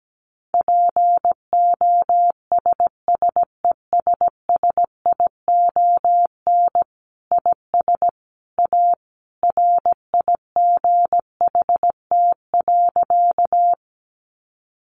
Morse code, heard as a keyed tone, about 17 wpm, 700 Hz. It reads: POSSESSION IS A RIGHT.
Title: Sentences from Top 1400 Words 17wpm